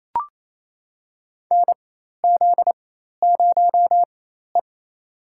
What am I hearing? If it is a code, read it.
Morse code, heard as a keyed tone, about 28 wpm, spaced as for 13 wpm, 700 Hz. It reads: NZ0E
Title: USA Call Signs - 4 Characters - 4x Character Spacing - 28wpm